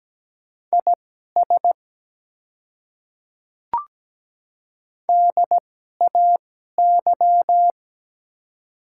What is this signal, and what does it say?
Morse code, heard as a keyed tone, about 17 wpm, spaced as for 12 wpm, 700 Hz. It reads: IS DAY
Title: 2x Character Spacing - Top 200 Words - 17wpm